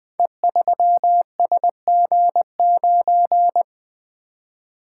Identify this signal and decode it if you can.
Morse code, heard as a keyed tone, about 20 wpm, 700 Hz. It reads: E3SG9
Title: Code Groups - 5 Letter:Numbers - 20wpm